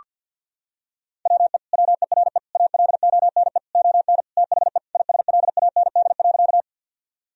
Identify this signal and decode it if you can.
Morse code, heard as a keyed tone, about 47 wpm, 700 Hz. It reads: WE WERE ALONE ON THE ISLAND.
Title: Sentences from Top 1000 Words - Review Part 3 - 50wpm